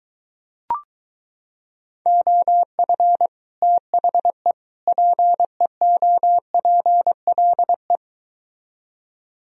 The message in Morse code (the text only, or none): OF THE PEOPLE